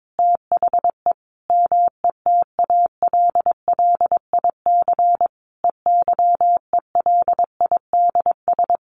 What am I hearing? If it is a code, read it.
Morse code, heard as a keyed tone, about 22 wpm, 700 Hz. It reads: THE METALLIC EYELIDS
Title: A Sound Of Thunder 22wpm